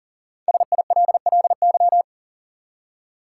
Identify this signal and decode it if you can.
Morse code, heard as a keyed tone, about 40 wpm, 700 Hz. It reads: SILLY
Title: Five Letter Words - 40wpm